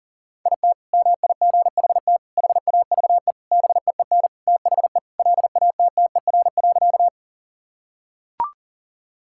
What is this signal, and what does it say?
Morse code, heard as a keyed tone, about 40 wpm, 700 Hz. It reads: IT MIGHT HAVE BEEN THE LATTER.
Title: Sentences from Top 1600 Words 40wpm